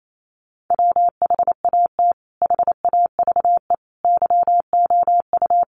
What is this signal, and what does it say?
Morse code, heard as a keyed tone, about 28 wpm, 700 Hz. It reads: WHAT HAVE YOU
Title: Sentences from Top 500 Words 28wpm